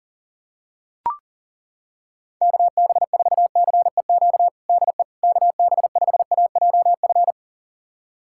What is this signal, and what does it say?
Morse code, heard as a keyed tone, about 40 wpm, 700 Hz. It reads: KB4CEQ DE KB5AJF
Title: USA Callsign DE Callsign 40wpm